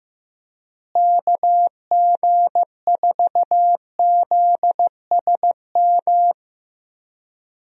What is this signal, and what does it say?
Morse code, heard as a keyed tone, about 15 wpm, 700 Hz. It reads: KG4ZSM
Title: USA Call Signs - 6 Characters - Repeated 2x - 15wpm